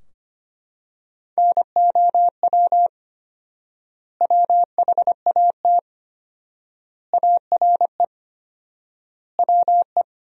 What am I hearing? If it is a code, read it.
Morse code, heard as a keyed tone, about 25 wpm, 700 Hz. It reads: NOW WHAT ARE WE
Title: 4x Word-Spacing - Sentences from Top 100 Words - 25wpm